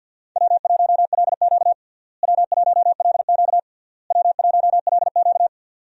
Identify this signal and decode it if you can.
Morse code, heard as a keyed tone, about 46 wpm, 700 Hz. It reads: W1LQ W1LQ W1LQ
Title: USA Call Signs - 4 Characters - Repeated 3x - 50wpm